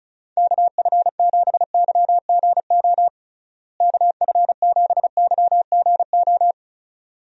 Morse code, heard as a keyed tone, about 36 wpm, 700 Hz. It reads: KF7YGO KF7YGO
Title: USA Call Signs - 6 Characters - Repeated 2x - 35wpm